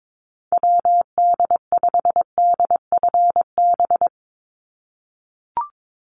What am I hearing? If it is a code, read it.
Morse code, heard as a keyed tone, about 22 wpm, 700 Hz. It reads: WD5DFB
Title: USA Call Signs 22wpm